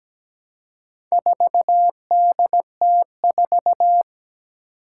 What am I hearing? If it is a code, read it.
Morse code, heard as a keyed tone, about 17 wpm, 700 Hz. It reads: 4DT4
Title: Four Challenging Letter-Number Combinations 17wpm